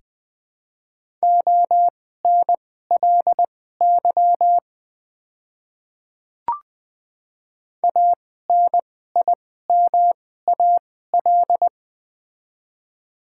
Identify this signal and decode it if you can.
Morse code, heard as a keyed tone, about 21 wpm, spaced as for 15 wpm, 700 Hz. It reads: ONLY ANIMAL
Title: 2x Character Spacing - Top 200 Words - 20wpm